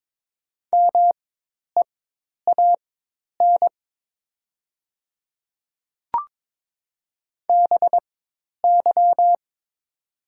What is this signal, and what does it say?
Morse code, heard as a keyed tone, about 22 wpm, spaced as for 10 wpm, 700 Hz. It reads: MEAN BY
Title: 4x Character Spacing - Top 200 Words - 22wpm